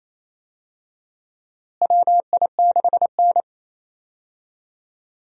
Morse code, heard as a keyed tone, about 28 wpm, 700 Hz. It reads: WI6N
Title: USA Call Signs - 4 Characters - Repeated 3x - 28wpm